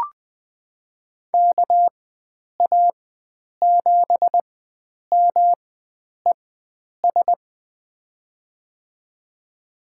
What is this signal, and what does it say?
Morse code, heard as a keyed tone, about 21 wpm, spaced as for 9 wpm, 700 Hz. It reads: KA7MES
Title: USA Call Signs - 6 Characters - 4x Character Spacing - 20wpm